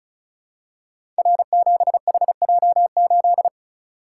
Morse code, heard as a keyed tone, about 36 wpm, 700 Hz. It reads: R7HJ8
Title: Code Groups - 5 Letter:Numbers - 35wpm